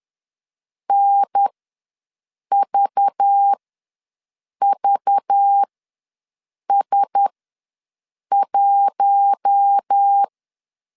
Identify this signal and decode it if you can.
Morse code, heard as a keyed tone, about 10 wpm, spaced as for 6 wpm, 800 Hz. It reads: NVVS1
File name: LEZIONI / 5 LBVC / 5 LBV ws 10wpm 3645